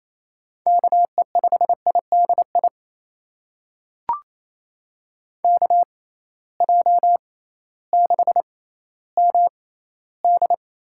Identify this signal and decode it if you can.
Morse code, heard as a keyed tone, about 28 wpm, spaced as for 15 wpm, 700 Hz. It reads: KE5IDI KJ6MD
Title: USA Call Signs - 6 Characters - 6x Character Spacing - 28wpm